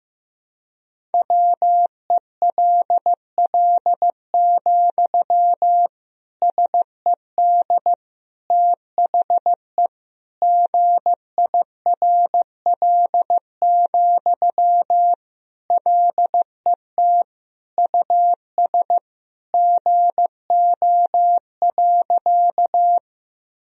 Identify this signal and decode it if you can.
Morse code, heard as a keyed tone, about 16 wpm, 700 Hz. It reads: WELL, SED THE GIRL, LET US GO.
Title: Sentences from Top 400 Words and Abbreviations 15wpm